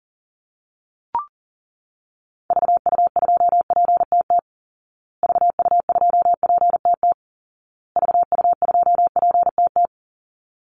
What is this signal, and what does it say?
Morse code, heard as a keyed tone, about 40 wpm, 700 Hz. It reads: VU2PTT VU2PTT VU2PTT
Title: Call Signs Repeated 2x - 40wpm